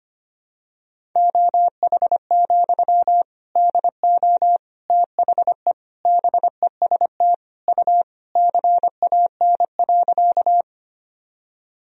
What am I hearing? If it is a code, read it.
Morse code, heard as a keyed tone, about 25 wpm, 700 Hz. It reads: OH, DO THE BEST U CAN.
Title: Sentences from Top 500 Words and Abbreviations 25wpm